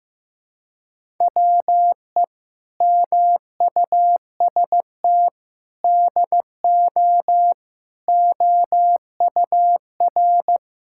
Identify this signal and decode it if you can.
Morse code, heard as a keyed tone, about 15 wpm, 700 Hz. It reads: WE MUST DO OUR
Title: Sentences from Top 1000 Words - Review Part 5 - 15wpm